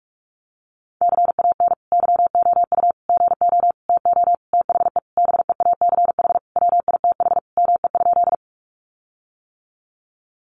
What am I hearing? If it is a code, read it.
Morse code, heard as a keyed tone, about 44 wpm, 700 Hz. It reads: CAN YOU GO TO THE BEACH WITH ME?
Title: Sentences from Top 1500 Words 45wpm